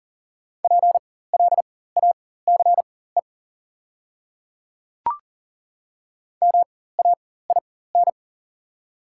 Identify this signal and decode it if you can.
Morse code, heard as a keyed tone, about 40 wpm, spaced as for 19 wpm, 700 Hz. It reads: PLACE MAIN
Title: 4x Character Spacing - Top 300 Words - 40wpm